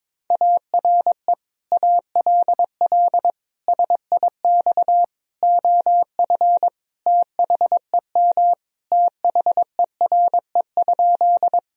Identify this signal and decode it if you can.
Morse code, heard as a keyed tone, about 21 wpm, 700 Hz. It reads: ARE ALL SIX OF THEM THERE?
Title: Sentences from Top 1000 Words - Review Part 2 - 22wpm